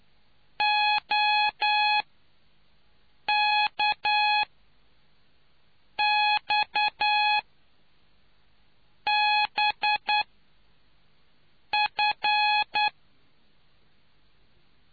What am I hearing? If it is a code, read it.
Morse code, heard as a keyed tone, about 9 wpm, spaced as for 4 wpm, 840 Hz. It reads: OKXBF